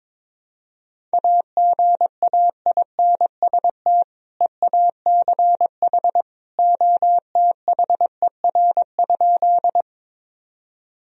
Morse code, heard as a keyed tone, about 22 wpm, 700 Hz. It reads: AGAINST EACH OTHER?
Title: Sentences from Top 1000 Words - Review Part 1 - 22wpm